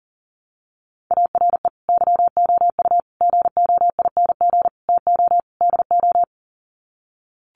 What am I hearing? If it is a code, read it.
Morse code, heard as a keyed tone, about 40 wpm, 700 Hz. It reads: ARE YOU GOING TO DO
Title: Sets of 5 Words 40wpm